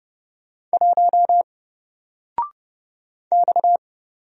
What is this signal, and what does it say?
Morse code, heard as a keyed tone, about 30 wpm, 700 Hz. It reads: 1 X
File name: Single-Letter-Number-30wpm